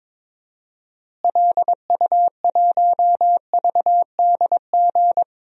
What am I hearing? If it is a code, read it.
Morse code, heard as a keyed tone, about 22 wpm, 700 Hz. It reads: LU1VDG
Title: DXCC-100 Call Signs - 6 Characters - Repeated 2x - 22wpm